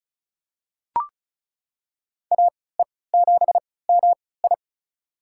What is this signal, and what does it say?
Morse code, heard as a keyed tone, about 34 wpm, spaced as for 20 wpm, 700 Hz. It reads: AE7MI
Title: USA Call Signs - 5 Characters - 3x Character Spacing - 35wpm